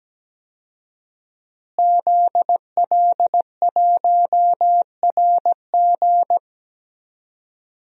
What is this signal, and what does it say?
Morse code, heard as a keyed tone, about 17 wpm, 700 Hz. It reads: ZL1RG
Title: DXCC-100 Call Signs - 5 Characters - Repeated 2x - 17wpm